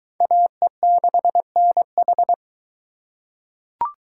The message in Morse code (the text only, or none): AE6NH